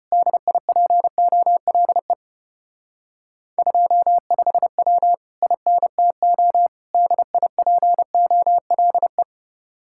Morse code, heard as a keyed tone, about 31 wpm, 700 Hz. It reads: DIPOLE 25W INTO DIPOLE
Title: Speed-Racing Common QSO Phrases - 25wpm